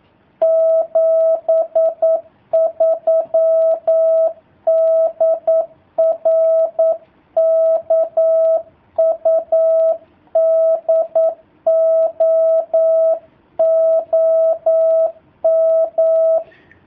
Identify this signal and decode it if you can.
Morse code, heard as a keyed tone, about 9 wpm, 660 Hz. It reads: 73 DR KUDO OM
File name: jh8cbh / 2019cw / sp09- w